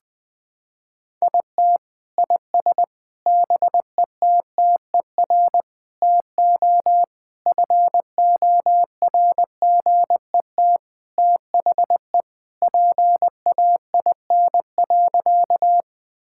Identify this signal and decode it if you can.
Morse code, heard as a keyed tone, about 20 wpm, 700 Hz. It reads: IT IS BETTER TO FORGET THE PAIN.
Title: Sentences from Top 1600 Words 20wpm